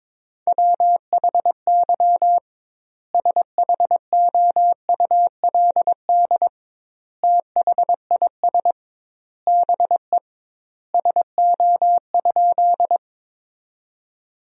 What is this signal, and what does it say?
Morse code, heard as a keyed tone, about 22 wpm, 700 Hz. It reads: WHY SHOULD THIS BE SO?
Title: 2x Word-Spacing - Sentences from Top 300 Words - 22wpm